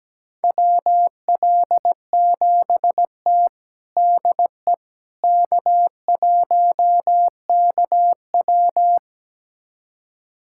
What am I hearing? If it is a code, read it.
Morse code, heard as a keyed tone, about 17 wpm, 700 Hz. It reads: WL7T DE K1KW
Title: USA Callsign DE Callsign - 4 Characters - 17wpm